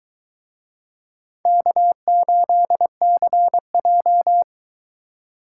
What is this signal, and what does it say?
Morse code, heard as a keyed tone, about 23 wpm, 700 Hz. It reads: K8CJ